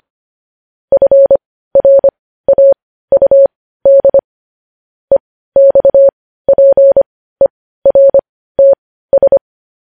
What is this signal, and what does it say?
Morse code, heard as a keyed tone, about 25 wpm, spaced as for 15 wpm, 550 Hz. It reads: FRAUD EXPERTS
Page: news-headlines-in-morse-code-at-15-wpm